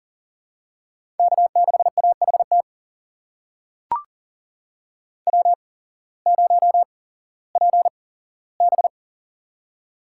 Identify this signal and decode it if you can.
Morse code, heard as a keyed tone, about 40 wpm, spaced as for 19 wpm, 700 Hz. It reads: K6AHT W0PB
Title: USA Call Signs - 8x Character Spacing - 40wpm